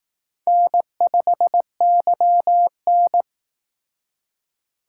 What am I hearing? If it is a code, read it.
Morse code, heard as a keyed tone, about 18 wpm, 700 Hz. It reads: N5YN
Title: Speed Racing - USA 1x2 Call Signs 15wpm